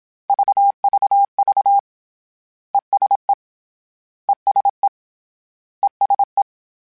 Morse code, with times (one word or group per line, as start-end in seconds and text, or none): 0.3-1.8 VVV
2.7-3.3 ESE
4.3-4.9 ESE
5.8-6.4 ESE